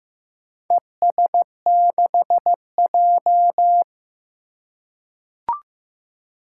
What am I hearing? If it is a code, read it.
Morse code, heard as a keyed tone, about 15 wpm, 700 Hz. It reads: ES6J
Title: Four Challenging Letter-Number Combinations 15wpm